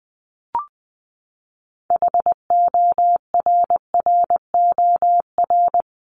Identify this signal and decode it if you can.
Morse code, heard as a keyed tone, about 20 wpm, 700 Hz. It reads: HORROR
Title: Top 1501-1750 Words 20wpm